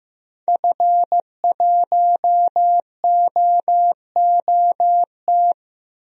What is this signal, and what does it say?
Morse code, heard as a keyed tone, about 15 wpm, 700 Hz. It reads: F1OOT
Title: DXCC-100 Call Signs - 5 Characters - Repeated 2x - 15wpm